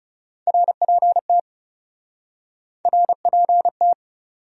Morse code, heard as a keyed tone, about 40 wpm, 700 Hz. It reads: RPT RPT